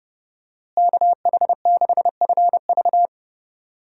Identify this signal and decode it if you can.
Morse code, heard as a keyed tone, about 32 wpm, 700 Hz. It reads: KH6FV
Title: USA Call Signs - 5 Characters - Repeated 2x - 30wpm